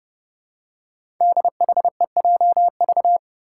Morse code, heard as a keyed tone, about 30 wpm, 700 Hz. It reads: DHEJV
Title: Five Challenging Letter-Number Combinations 30wpm